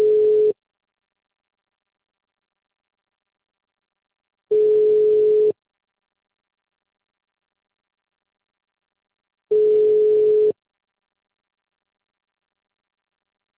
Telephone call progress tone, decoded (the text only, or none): ringback tone